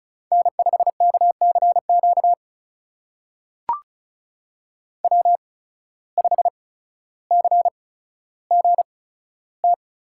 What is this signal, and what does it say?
Morse code, heard as a keyed tone, about 36 wpm, spaced as for 15 wpm, 700 Hz. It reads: N5KCQ W5CGT
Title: USA Call Signs - 5 Characters - 8x Character Spacing - 35wpm